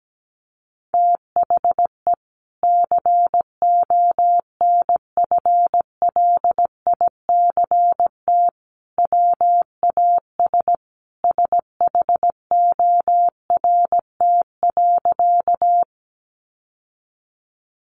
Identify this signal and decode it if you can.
Morse code, heard as a keyed tone, about 17 wpm, 700 Hz. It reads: THE CONFLICT WAS SHORT.
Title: Sentences from Top 1300 Words 17wpm